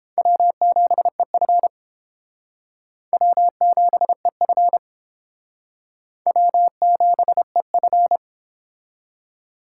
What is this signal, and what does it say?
Morse code, heard as a keyed tone, about 32 wpm, 700 Hz. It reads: W7EF W7EF W7EF